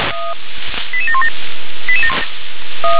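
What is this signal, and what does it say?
Five-tone selective call (ZVEI), heard as quick runs of five tones, tones of about 80 ms. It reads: 88717
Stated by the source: ZVEI five-tone selective call received on SDR.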